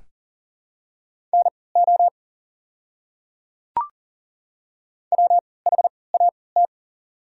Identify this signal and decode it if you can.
Morse code, heard as a keyed tone, about 37 wpm, spaced as for 22 wpm, 690 Hz. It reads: NO WHAT
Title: 3x Character Spacing - Top 100 Words - 40wpm